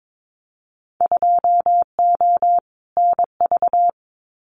Morse code, heard as a keyed tone, about 22 wpm, 700 Hz. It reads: 2O NV
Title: Winter Field Day Exchanges 22wpm